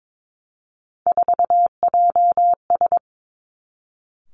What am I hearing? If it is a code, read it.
Morse code, heard as a keyed tone, about 22 wpm, 700 Hz. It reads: 4JS
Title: Three Challenging Letter-Number Combinations 22wpm